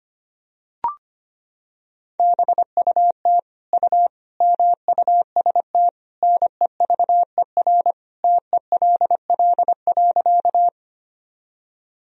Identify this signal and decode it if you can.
Morse code, heard as a keyed tone, about 25 wpm, 700 Hz. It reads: BUT U MUST NEVER TELL.
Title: Sentences from Top 300 Words and Abbreviations 25wpm